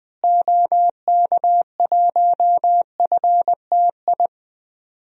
Morse code, heard as a keyed tone, about 20 wpm, 700 Hz. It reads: OK1FTI
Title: DXCC-100 Call Signs - 6 Characters - Repeated 2x - 20wpm